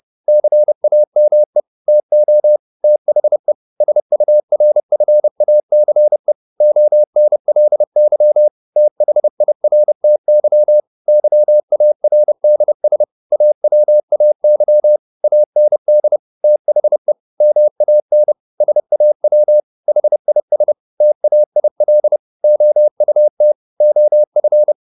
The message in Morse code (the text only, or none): CAME TO THE SURFACE ONLY THIRTY YARDS AWAY AND THE MAN SAW HIS TAIL OUT OF